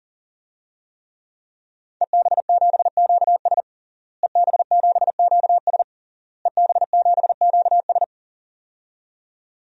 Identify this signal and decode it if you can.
Morse code, heard as a keyed tone, about 40 wpm, 700 Hz. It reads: EB7QS EB7QS EB7QS
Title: DXCC-100 Call Signs - 5 Characters - Repeated 3x - 40wpm